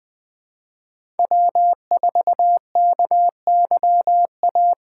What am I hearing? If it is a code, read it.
Morse code, heard as a keyed tone, about 20 wpm, 700 Hz. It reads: W4KYA
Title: USA Call Signs - 2x Character Spacing - 20wpm